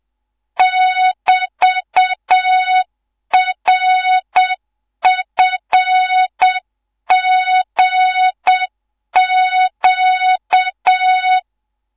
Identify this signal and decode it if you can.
Morse code, heard as a keyed tone, about 7 wpm, 3700 Hz. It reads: <BT>RFGQ